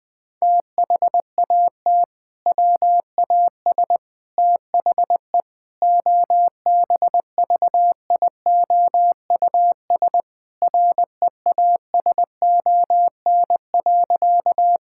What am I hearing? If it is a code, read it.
Morse code, heard as a keyed tone, about 20 wpm, 700 Hz. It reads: THAT WAS THE OBVIOUS REASON.
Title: Sentences from Top 1600 Words 20wpm